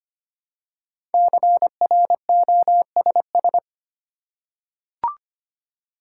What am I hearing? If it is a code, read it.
Morse code, heard as a keyed tone, about 25 wpm, 700 Hz. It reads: CROSS
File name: 5 Letter Words - First 3 Letters Given - 25wpm